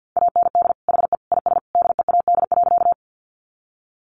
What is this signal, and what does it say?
Morse code, heard as a keyed tone, about 50 wpm, 700 Hz. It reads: AND HE IS DEAD.
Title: Sentences from Top 1500 Words 50wpm